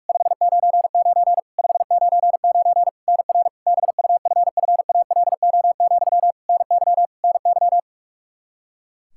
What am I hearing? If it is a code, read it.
Morse code, heard as a keyed tone, about 46 wpm, 700 Hz. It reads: 599 599 NR BUFFALO, NY NY
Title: K1USN SST QSO Practice 45wpm